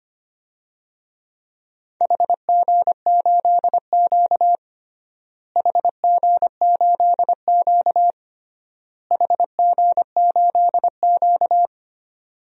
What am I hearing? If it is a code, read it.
Morse code, heard as a keyed tone, about 25 wpm, 700 Hz. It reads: HG8Q HG8Q HG8Q
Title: DXCC-100 Call Signs - 4 Characters - Repeated 3x - 25wpm